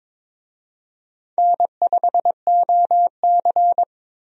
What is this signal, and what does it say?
Morse code, heard as a keyed tone, about 22 wpm, 700 Hz. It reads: N5OC
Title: USA Call Signs - 4 Characters - Repeated 2x - 22wpm